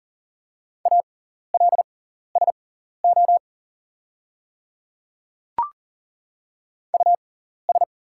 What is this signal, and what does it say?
Morse code, heard as a keyed tone, about 40 wpm, spaced as for 14 wpm, 700 Hz. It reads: ALSO US
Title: 6x Character Spacing - Top 300 Words - 40wpm